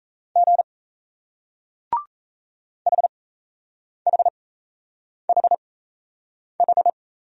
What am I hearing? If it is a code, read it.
Morse code, heard as a keyed tone, about 38 wpm, spaced as for 8 wpm, 700 Hz. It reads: G HHHH